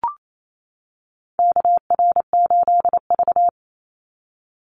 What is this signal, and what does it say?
Morse code, heard as a keyed tone, about 28 wpm, 700 Hz. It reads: KR8V
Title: Call Signs - 28wpm